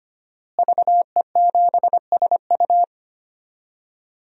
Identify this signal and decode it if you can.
Morse code, heard as a keyed tone, about 25 wpm, 700 Hz. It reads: VE7SU
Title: DXCC-100 Call Signs - 5 Characters - Repeated 2x - 25wpm